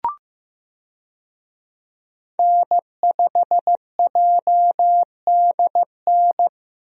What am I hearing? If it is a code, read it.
Morse code, heard as a keyed tone, about 15 wpm, 700 Hz. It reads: N5JDN
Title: USA Call Signs - 5 Characters - Repeated 3x - 15wpm